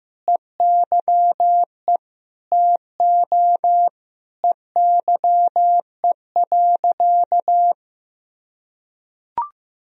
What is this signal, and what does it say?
Morse code, heard as a keyed tone, about 15 wpm, 700 Hz. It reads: EYE TO EYE.
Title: Sentences from Top 1000 Words - Review Part 2 - 15wpm